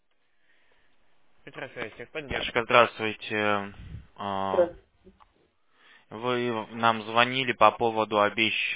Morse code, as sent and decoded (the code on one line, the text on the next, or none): none